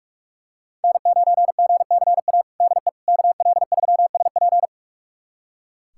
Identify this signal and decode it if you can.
Morse code, heard as a keyed tone, about 46 wpm, 700 Hz. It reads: N9GCA DE KL3SP
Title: USA Callsign DE Callsign - 5 Characters - 45wpm